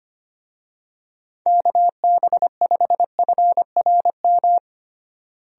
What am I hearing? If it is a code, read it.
Morse code, heard as a keyed tone, about 25 wpm, 700 Hz. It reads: KB5FRM
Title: USA Call Signs - 6 Characters - Repeated 2x - 25wpm